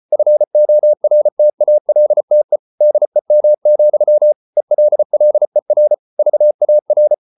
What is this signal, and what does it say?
Morse code, heard as a keyed tone, about 34 wpm, 600 Hz. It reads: FORTALTE DEM, ELLER VAR
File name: Ferie QRQ 34wpm